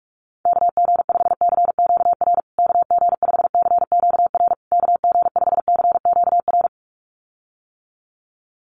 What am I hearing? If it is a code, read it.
Morse code, heard as a keyed tone, about 44 wpm, 700 Hz. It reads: KG5CQR KG5CQR KG5CQR
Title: USA Call Signs Repeated 3x 45wpm